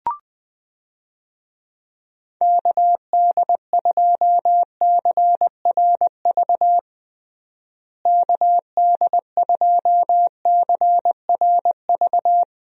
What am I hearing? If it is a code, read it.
Morse code, heard as a keyed tone, about 20 wpm, 700 Hz. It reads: KD2CRV KD2CRV
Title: USA Call Signs - 6 Characters - Repeated 2x - 20wpm